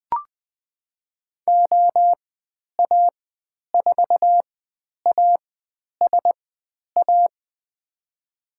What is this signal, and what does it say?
Morse code, heard as a keyed tone, about 20 wpm, spaced as for 10 wpm, 700 Hz. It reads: OA4ASA
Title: 102 Lesson 17g - Letter u - Call Signs 20wpm